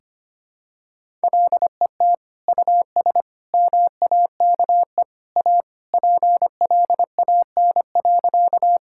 Morse code, heard as a keyed tone, about 25 wpm, 700 Hz. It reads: LET US MAKE A PLAN.
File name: practice / 25wpm / Sentences with 2-Word Phrases - 25wpm